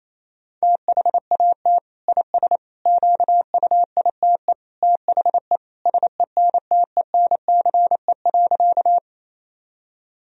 Morse code, heard as a keyed tone, about 28 wpm, 700 Hz. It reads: THAT IS QUITE THE SENTENCE.
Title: Sentences from Top 1000 Words - Review Part 1 - 28wpm